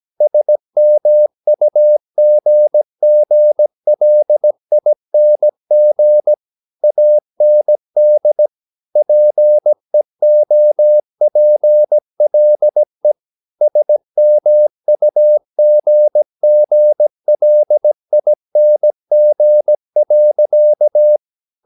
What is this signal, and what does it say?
Morse code, heard as a keyed tone, about 17 wpm, 600 Hz. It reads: SMUGGLING AND PEOPLE SMUGGLING.
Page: Romanen Never af Ken Follet på engelsk.